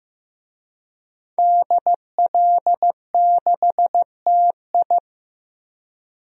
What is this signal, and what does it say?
Morse code, heard as a keyed tone, about 16 wpm, 700 Hz. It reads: DL6TI
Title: DXCC-100 Call Signs - 5 Characters - Repeated 2x - 15wpm